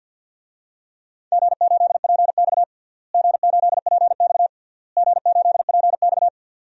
Morse code, heard as a keyed tone, about 46 wpm, 700 Hz. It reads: G8PX G8PX G8PX
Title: DXCC-100 Call Signs - 4 Characters - Repeated 3x - 50wpm